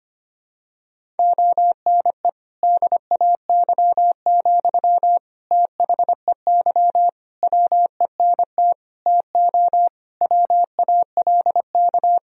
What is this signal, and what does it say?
Morse code, heard as a keyed tone, about 25 wpm, 700 Hz. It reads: ONE DAY, THEY WENT TO WALK
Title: Sentences from Top 1000 Words - Review Part 2 - 25wpm